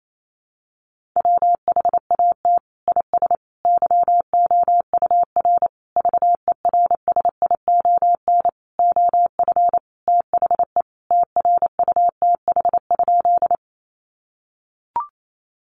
Morse code, heard as a keyed tone, about 28 wpm, 700 Hz. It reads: WHAT IS YOUR VERSION OF THE TRUTH?
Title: Sentences from Top 1400 Words 28wpm